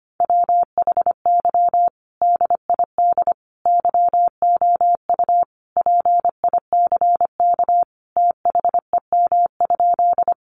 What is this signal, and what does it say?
Morse code, heard as a keyed tone, about 25 wpm, 700 Hz. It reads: WHY DID YOU PICK THEM?
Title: Sentences from Top 1000 Words 25wpm